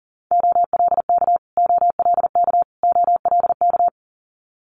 Morse code, heard as a keyed tone, about 38 wpm, 700 Hz. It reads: OLK OLK OLK
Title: Multi-letter Phonemes Repeated 3x 40wpm